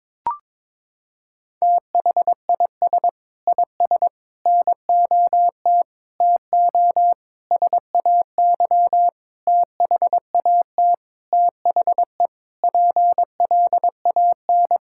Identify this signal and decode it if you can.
Morse code, heard as a keyed tone, about 22 wpm, 700 Hz. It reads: THIS IS NOT TO SAY THAT THE PLAN
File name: Sentences with 7-Word Phrases - 22wpm